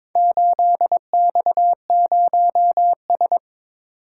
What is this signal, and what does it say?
Morse code, heard as a keyed tone, about 22 wpm, 700 Hz. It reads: 8X0S
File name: practice / 22wpm / Four Letter-Number Combinations - 22wpm